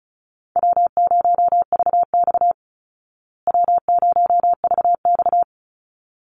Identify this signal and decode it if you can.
Morse code, heard as a keyed tone, about 34 wpm, 700 Hz. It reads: W0VX W0VX
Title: Call Signs Repeated 3x - 35wpm